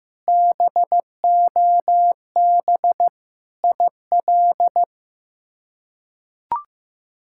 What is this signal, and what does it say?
Morse code, heard as a keyed tone, about 15 wpm, 700 Hz. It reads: BOB IL
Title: K1USN SST QSO Practice - 15wpm